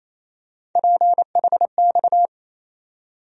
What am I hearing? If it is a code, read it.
Morse code, heard as a keyed tone, about 28 wpm, 700 Hz. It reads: PHX
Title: USA Call Sign Postfixes - 3 Letters 28wpm